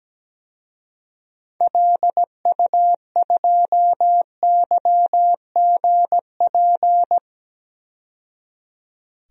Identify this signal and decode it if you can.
Morse code, heard as a keyed tone, about 17 wpm, 700 Hz. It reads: LU2YGP
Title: DXCC-100 Call Signs - 6 Characters - 17wpm